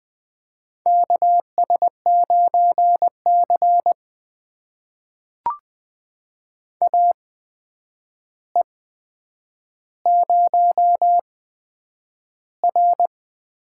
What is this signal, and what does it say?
Morse code, heard as a keyed tone, about 20 wpm, spaced as for 9 wpm, 700 Hz. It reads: KS9C AE0R
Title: USA Call Signs - 4 Characters - 8x Character Spacing - 20wpm